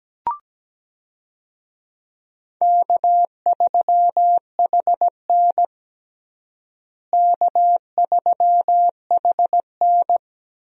Morse code, heard as a keyed tone, about 17 wpm, 700 Hz. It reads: K3HN K3HN
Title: USA Call Signs - 4 Characters - Repeated 3x - 17wpm